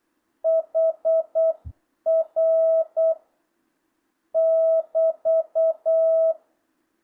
Morse code, heard as a keyed tone, about 8 wpm, 650 Hz. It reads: HR <BT>
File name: jh8cbh / 2025cw